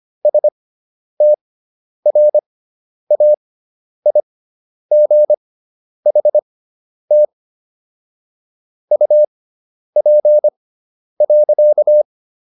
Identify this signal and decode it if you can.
Morse code, heard as a keyed tone, about 25 wpm, spaced as for 10 wpm, 600 Hz. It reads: STRAIGHT UP.